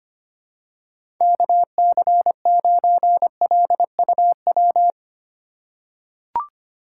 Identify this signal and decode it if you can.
Morse code, heard as a keyed tone, about 25 wpm, 700 Hz. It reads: KC9LUW
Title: USA Call Signs - 6 Characters - 4x Character Spacing - 25wpm